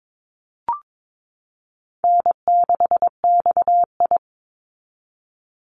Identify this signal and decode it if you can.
Morse code, heard as a keyed tone, about 22 wpm, 700 Hz. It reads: N6XI
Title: Easy Call Signs Repeated 3x - 22wpm